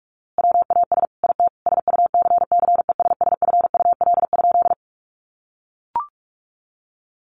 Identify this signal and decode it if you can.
Morse code, heard as a keyed tone, about 44 wpm, 700 Hz. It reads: WAS IT SUCCESSFUL?
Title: A Sound Of Thunder 45wpm